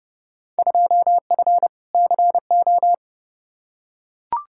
Morse code, heard as a keyed tone, about 31 wpm, 700 Hz. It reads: 2F CO
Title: Field Day Exchanges - 30wpm